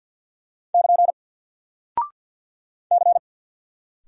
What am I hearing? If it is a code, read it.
Morse code, heard as a keyed tone, about 48 wpm, 700 Hz. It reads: <KN> C
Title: Single Character (Letter:Number:Punctuation:Prosign) Alt 50wpm